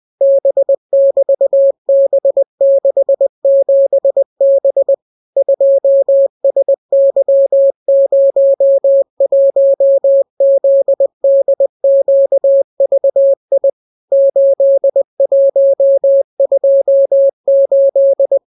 Morse code, heard as a keyed tone, about 20 wpm, 550 Hz. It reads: B<BT>B67B 2SY01ZDQVI 8128